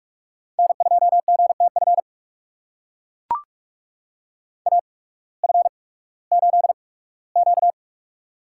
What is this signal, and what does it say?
Morse code, heard as a keyed tone, about 41 wpm, spaced as for 20 wpm, 700 Hz. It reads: N2GTF AF8Q
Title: USA Call Signs - 8x Character Spacing - 45wpm